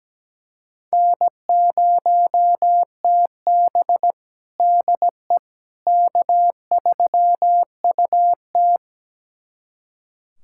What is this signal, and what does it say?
Morse code, heard as a keyed tone, about 17 wpm, 700 Hz. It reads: N0TB DE K3UT
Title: USA Callsign DE Callsign - 4 Characters - 17wpm